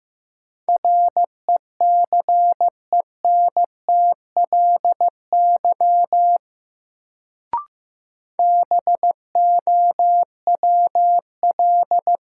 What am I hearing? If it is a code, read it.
Morse code, heard as a keyed tone, about 15 wpm, 700 Hz. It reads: RECENTLY BOWL
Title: Sentences from Top 1200 Words - New Word Drill - 15wpm